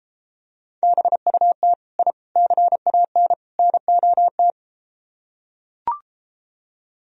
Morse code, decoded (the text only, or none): BUT I CAN NOT